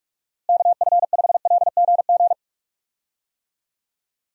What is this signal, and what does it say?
Morse code, heard as a keyed tone, about 46 wpm, 700 Hz. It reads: KF5LGG
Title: USA Call Signs - 6 Characters - Repeated 2x - 45wpm